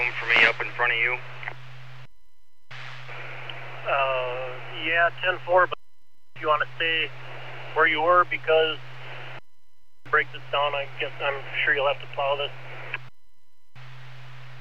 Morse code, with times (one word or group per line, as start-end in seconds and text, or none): none